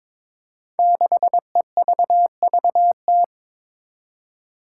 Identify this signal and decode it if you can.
Morse code, heard as a keyed tone, about 22 wpm, 700 Hz. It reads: 6EVVT
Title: Five Challenging Letter-Number Combinations 22wpm